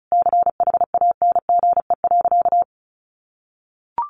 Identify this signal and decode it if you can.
Morse code, heard as a keyed tone, about 34 wpm, 700 Hz. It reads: CHANGE.
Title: Sentences from Top 1500 Words 35wpm